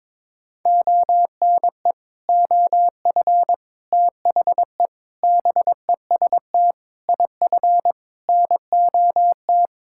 Morse code, heard as a keyed tone, about 22 wpm, 700 Hz. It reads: ONE OF THE BEST IF NOT
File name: practice / 22wpm / Sentences with 8-Word Phrases - 22wpm